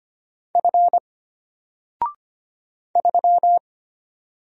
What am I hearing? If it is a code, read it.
Morse code, heard as a keyed tone, about 25 wpm, 700 Hz. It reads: F 3
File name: Single-Letter-Number-25wpm